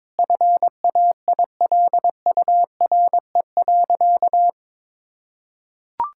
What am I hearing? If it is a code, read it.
Morse code, heard as a keyed tone, about 22 wpm, 700 Hz. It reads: FAILURE.
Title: Sentences from Top 1000 Words - Review Part 6 - 22wpm